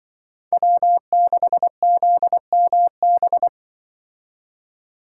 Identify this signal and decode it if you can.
Morse code, heard as a keyed tone, about 24 wpm, 700 Hz. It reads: W6ZMB
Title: Speed Racing - USA 1x3 Call Signs 20wpm